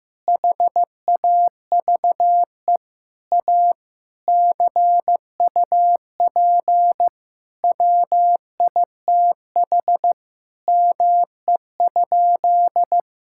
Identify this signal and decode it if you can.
Morse code, heard as a keyed tone, about 15 wpm, 700 Hz. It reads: HAVE A CUP WITH ME?
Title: Sentences from Top 1600 Words 15wpm